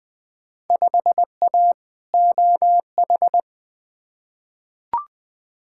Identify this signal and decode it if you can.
Morse code, heard as a keyed tone, about 20 wpm, 700 Hz. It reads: 5A OH
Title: Field Day Exchanges Encore - 20wpm